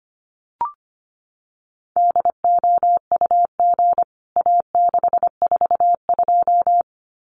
Morse code, heard as a keyed tone, about 25 wpm, 700 Hz. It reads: DOUG A642
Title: CWT Exchange 25wpm